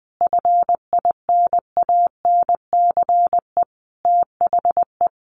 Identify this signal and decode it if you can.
Morse code, heard as a keyed tone, about 20 wpm, 700 Hz. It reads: FINANCE THE
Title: Sentences from Top 1500 Words 20wpm